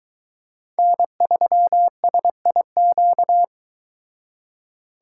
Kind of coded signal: Morse code

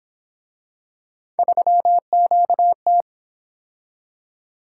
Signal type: Morse code

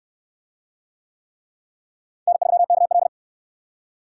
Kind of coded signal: Morse code

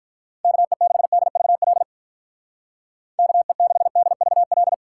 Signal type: Morse code

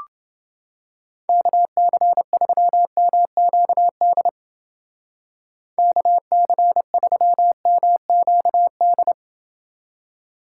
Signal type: Morse code

